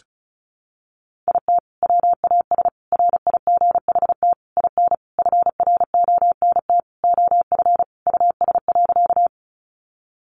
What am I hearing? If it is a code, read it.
Morse code, heard as a keyed tone, about 34 wpm, 700 Hz. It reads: IT WAS RIGHT IN FRONT OF US.
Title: Sentences from Top 500 Words 35wpm